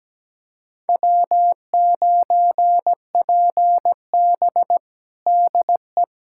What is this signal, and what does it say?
Morse code, heard as a keyed tone, about 17 wpm, 700 Hz. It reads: W9PB DE
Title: USA Callsign DE Callsign - 4 Characters - 17wpm